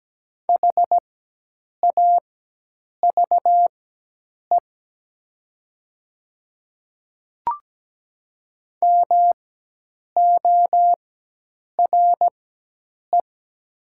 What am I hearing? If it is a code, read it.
Morse code, heard as a keyed tone, about 17 wpm, spaced as for 8 wpm, 700 Hz. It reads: HAVE MORE
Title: 4x Character Spacing - Top 100 Words - 17wpm